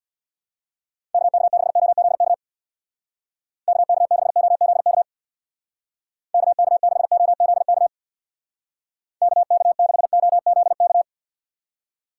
Morse code, heard as a keyed tone, about 60 wpm, 700 Hz. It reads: KK6OZK KK6OZK KK6OZK KK6OZK
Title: Speed Racing - USA 2x3 Call Signs 50wpm